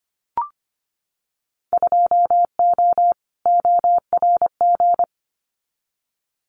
Morse code, heard as a keyed tone, about 25 wpm, 690 Hz. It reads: 2O ORG
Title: Winter Field Day Exchanges 25wpm